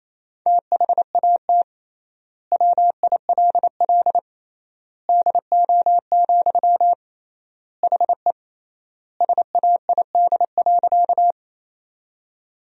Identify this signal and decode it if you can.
Morse code, heard as a keyed tone, about 28 wpm, 700 Hz. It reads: THAT WILL DO, HE SAID.
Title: 3x Word-Spacing - Sentences from Top 100 Words - 28wpm